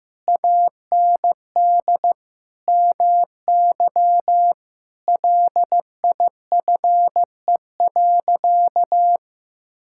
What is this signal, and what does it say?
Morse code, heard as a keyed tone, about 15 wpm, 700 Hz. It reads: AND MY LIFE.
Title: Sentences from Top 1000 Words - Review Part 5 - 15wpm